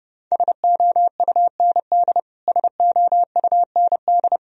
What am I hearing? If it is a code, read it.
Morse code, heard as a keyed tone, about 32 wpm, 700 Hz. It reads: SOUND SOUND
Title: Top 100 Words Repeated 30wpm